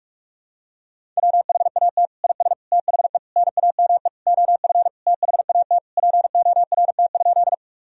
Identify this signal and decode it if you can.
Morse code, heard as a keyed tone, about 43 wpm, 700 Hz. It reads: WHAT IS THE NAME OF THAT PORT?
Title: Sentences from Top 1000 Words - Review Part 1 - 45wpm